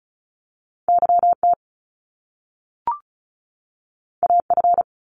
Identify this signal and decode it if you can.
Morse code, heard as a keyed tone, about 36 wpm, 700 Hz. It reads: Y T A F
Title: Top Two Letter Combinations in Words 35wpm